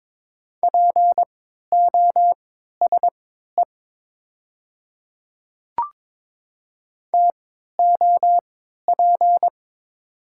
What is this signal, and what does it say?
Morse code, heard as a keyed tone, about 22 wpm, spaced as for 13 wpm, 700 Hz. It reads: POSE TOP
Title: 3x Character Spacing - Top 500 Words - 22wpm